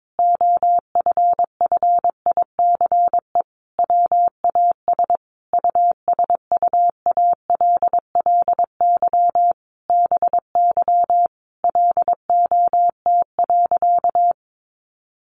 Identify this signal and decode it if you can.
Morse code, heard as a keyed tone, about 22 wpm, 700 Hz. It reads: OFFICE WAS USUALLY BY LOT.
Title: Sentences from Top 1500 Words 22wpm